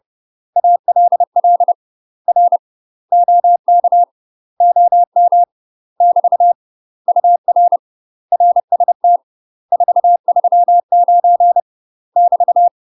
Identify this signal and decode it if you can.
Morse code, heard as a keyed tone, about 30 wpm, 700 Hz. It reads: ALL R OK OM <BT> UR RST 439 <BT>
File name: UK MASTER QSOs TRAINING / 30WPM / UK MASTER QSOs 0031